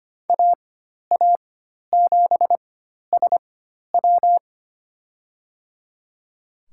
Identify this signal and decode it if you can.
Morse code, heard as a keyed tone, about 25 wpm, spaced as for 12 wpm, 700 Hz. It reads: AA7SW
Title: USA Call Signs - 5 Characters - 4x Character Spacing - 25wpm